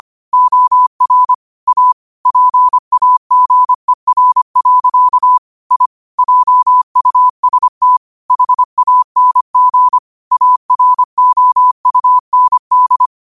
Morse code, heard as a keyed tone, about 25 wpm, 1000 Hz. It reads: OR A PAGER. I JUST HANG AROUND